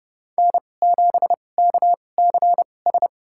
Morse code, heard as a keyed tone, about 32 wpm, spaced as for 22 wpm, 700 Hz. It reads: N7KCS
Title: USA Call Signs - 5 Characters - 2x Character Spacing - 30wpm